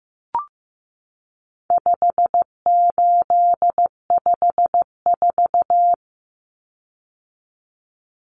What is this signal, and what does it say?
Morse code, heard as a keyed tone, about 14 wpm, 700 Hz. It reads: 5854
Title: Common Cut Numbers - 4-Digit 15wpm